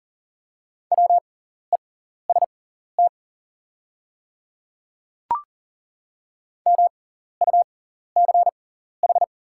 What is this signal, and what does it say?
Morse code, heard as a keyed tone, about 37 wpm, spaced as for 14 wpm, 700 Hz. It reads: WEST MUCH
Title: 6x Character Spacing - Top 500 Words - 40wpm